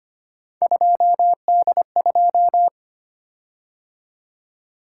Morse code, heard as a keyed tone, about 25 wpm, 700 Hz. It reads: 2D2
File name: Three Letter-Number Combinations - 25wpm